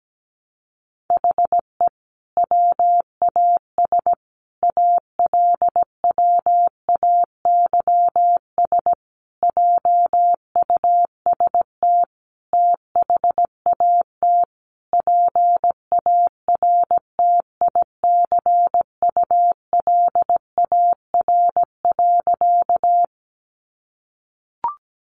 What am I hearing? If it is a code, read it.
Morse code, heard as a keyed tone, about 17 wpm, 700 Hz. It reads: HE WAS ALWAYS JUST THAT PARTICULAR.
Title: Sentences from Top 600 Words 17wpm